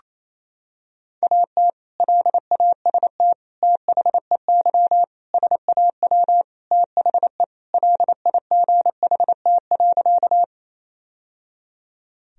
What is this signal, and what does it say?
Morse code, heard as a keyed tone, about 28 wpm, 700 Hz. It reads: AT LAST THEY SAW THE LIGHT.
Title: Sentences from Top 1000 Words - Review Part 1 - 28wpm